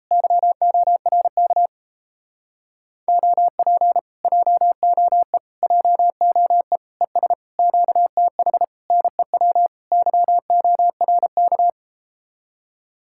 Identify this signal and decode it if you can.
Morse code, heard as a keyed tone, about 37 wpm, 700 Hz. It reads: YORK OP JOE JOE ES QTH NEW YORK